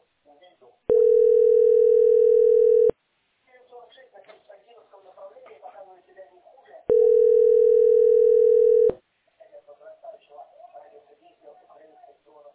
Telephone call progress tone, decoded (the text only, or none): ringback tone